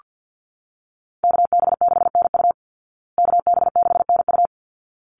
Morse code, heard as a keyed tone, about 48 wpm, 700 Hz. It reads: KB6NU KB6NU
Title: Call Signs Repeated 3x - 50wpm